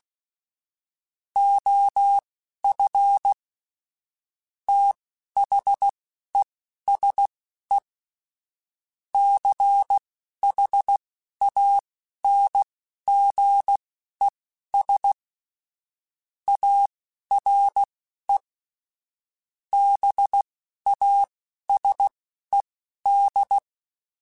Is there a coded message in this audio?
Yes: Morse code from 1.4 to 23.6 s, heard as a keyed tone, about 16 wpm, spaced as for 11 wpm, 800 Hz. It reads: OF THESE CHANGES ARE BASED